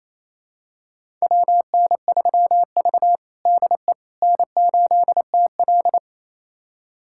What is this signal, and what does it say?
Morse code, heard as a keyed tone, about 28 wpm, 700 Hz. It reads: WN3V DE N8TL
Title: USA Callsign DE Callsign - 4 Characters - 28wpm